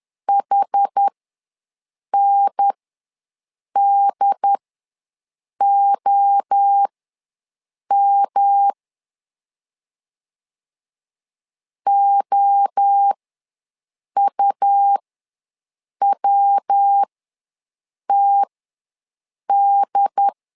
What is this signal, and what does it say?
Morse code, heard as a keyed tone, about 10 wpm, spaced as for 6 wpm, 800 Hz. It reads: HNDOM OUWTD
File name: LEZIONI / 3 MWRO / 3 MWRO ws 10wpm 2310